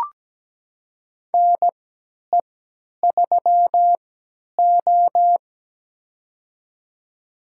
Morse code, heard as a keyed tone, about 17 wpm, spaced as for 10 wpm, 700 Hz. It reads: NE3O
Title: USA Call Signs - 4 Characters - 3x Character Spacing - 17wpm